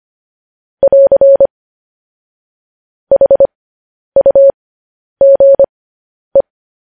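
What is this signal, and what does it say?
Morse code, heard as a keyed tone, about 25 wpm, spaced as for 10 wpm, 550 Hz. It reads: <AR> HUGE
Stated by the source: Morse code news headlines sent at 10.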